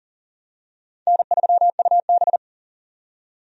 Morse code, heard as a keyed tone, about 40 wpm, 700 Hz. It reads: N3UB
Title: USA Call Signs - 4 Characters - 40wpm